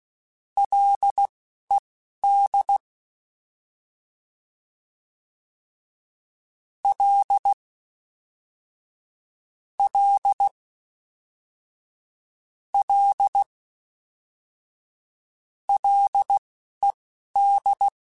Morse code, heard as a keyed tone, about 16 wpm, spaced as for 12 wpm, 800 Hz. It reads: LED L L L LED
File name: Ls / Single L m3s / Single L Prefixe 3x2 Twice @10